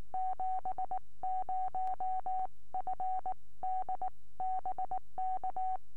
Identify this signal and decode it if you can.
Morse code, heard as a keyed tone, about 19 wpm, spaced as for 16 wpm, 750 Hz. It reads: 70FD BK
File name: ok1mac 4m